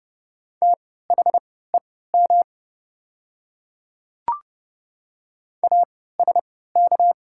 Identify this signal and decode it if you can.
Morse code, heard as a keyed tone, about 30 wpm, spaced as for 17 wpm, 690 Hz. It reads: THEM ASK
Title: 3x Character Spacing - Top 200 Words - 30wpm